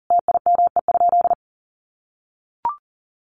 Morse code, heard as a keyed tone, about 37 wpm, 700 Hz. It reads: TIME?
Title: Sentences from Top 1200 Words 40wpm